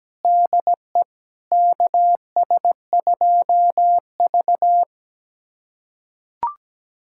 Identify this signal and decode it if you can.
Morse code, heard as a keyed tone, about 17 wpm, 700 Hz. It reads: DE KS2V
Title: USA Callsign DE Callsign - 4 Characters - 17wpm